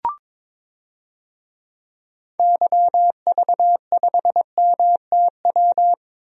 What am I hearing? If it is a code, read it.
Morse code, heard as a keyed tone, about 22 wpm, 700 Hz. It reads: YV5MTW
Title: DXCC-100 Call Signs - 6 Characters - Repeated 2x - 22wpm